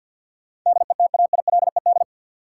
Morse code, heard as a keyed tone, about 50 wpm, 700 Hz. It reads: DETAILED
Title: Sentences from Top 1500 Words - New Word Drill Alt - 50wpm